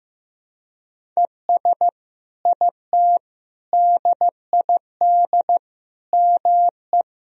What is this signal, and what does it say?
Morse code, heard as a keyed tone, about 15 wpm, 700 Hz. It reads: ES IT DID ME
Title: Sentences from Top 200 Words and Abbreviations 15wpm